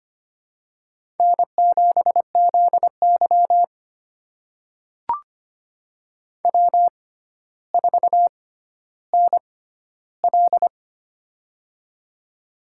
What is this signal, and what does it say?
Morse code, heard as a keyed tone, about 25 wpm, spaced as for 13 wpm, 700 Hz. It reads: N7ZY W4NL
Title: USA Call Signs - 4 Characters - 6x Character Spacing - 25wpm